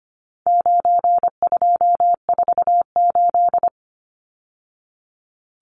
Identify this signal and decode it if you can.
Morse code, heard as a keyed tone, about 25 wpm, 700 Hz. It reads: 9248
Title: Numbers - 4-Digit 25wpm